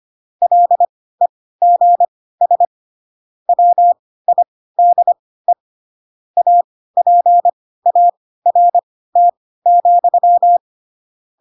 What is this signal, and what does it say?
Morse code, heard as a keyed tone, about 25 wpm, spaced as for 16 wpm, 700 Hz. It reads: LEGS WIDE APART,
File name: Pinocchio / Pinocchio ch3 10 25 ik1hge